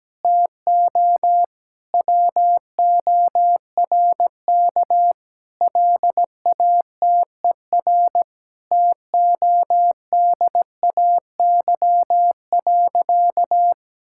Morse code, heard as a keyed tone, about 17 wpm, 700 Hz. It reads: TO WORK LATER TODAY.
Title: Sentences from Top 1000 Words - Review Part 6 - 17wpm